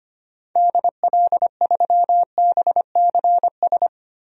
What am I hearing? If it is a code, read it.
Morse code, heard as a keyed tone, about 25 wpm, 700 Hz. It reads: DL3BCS
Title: DXCC-100 Call Signs - 6 Characters - 25wpm